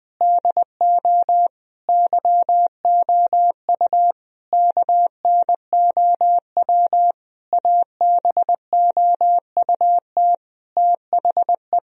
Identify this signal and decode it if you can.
Morse code, heard as a keyed tone, about 20 wpm, 700 Hz. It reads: DO YOU KNOW ABOUT THE
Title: Sentences from Top 1000 Words - Review Part 3 - 20wpm